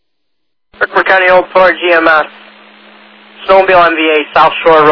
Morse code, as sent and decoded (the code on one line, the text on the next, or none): none